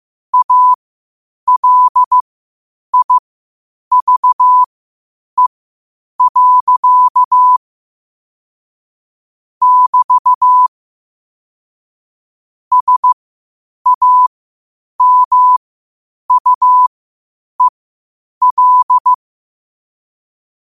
Morse code, heard as a keyed tone, about 15 wpm, spaced as for 8 wpm, 1000 Hz. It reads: ALIVE. <BT> SAMUEL
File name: qotdfiles / 8wpm